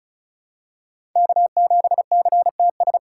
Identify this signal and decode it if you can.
Morse code, heard as a keyed tone, about 36 wpm, 700 Hz. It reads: K7CTS
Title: USA Call Signs - 5 Characters - 4x Character Spacing - 35wpm